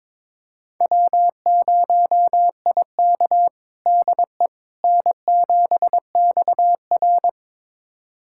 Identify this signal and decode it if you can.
Morse code, heard as a keyed tone, about 22 wpm, 700 Hz. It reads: W0IK DE N7XR
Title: USA Callsign DE Callsign - 4 Characters - 22wpm